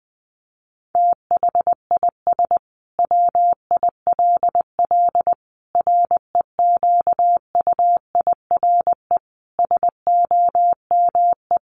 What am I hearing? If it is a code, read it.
Morse code, heard as a keyed tone, about 20 wpm, 700 Hz. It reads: THIS WILL REQUIRE SOME
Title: Sentences from Top 900 Words 20wpm